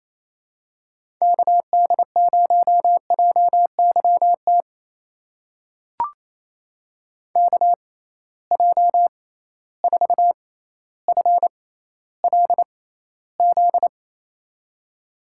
Morse code, heard as a keyed tone, about 27 wpm, spaced as for 14 wpm, 700 Hz. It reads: KD0JYT KJ4FLZ
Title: USA Call Signs - 6 Characters - 6x Character Spacing - 28wpm